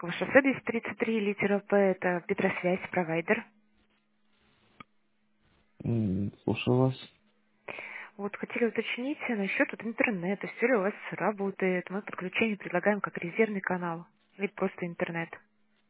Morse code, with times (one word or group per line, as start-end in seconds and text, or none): none